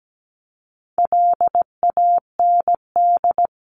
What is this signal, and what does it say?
Morse code, heard as a keyed tone, about 17 wpm, 700 Hz. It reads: LAND
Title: Top 500 Words 17wpm